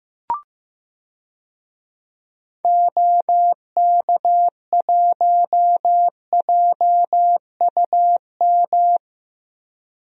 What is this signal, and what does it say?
Morse code, heard as a keyed tone, about 15 wpm, 700 Hz. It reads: OK1JUM
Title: DXCC-100 Call Signs - 6 Characters - Repeated 2x - 15wpm